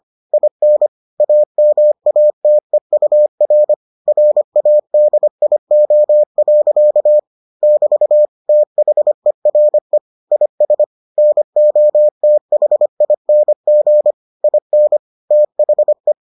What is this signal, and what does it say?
Morse code, heard as a keyed tone, about 25 wpm, 600 Hz. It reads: IN AMATEUR RADIO. <BT> THERE IS NOTHING IN THE